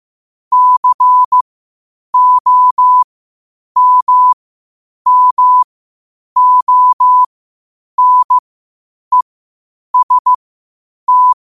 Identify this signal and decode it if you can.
Morse code, heard as a keyed tone, about 15 wpm, spaced as for 8 wpm, 1000 Hz. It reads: COMMONEST
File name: 8wpm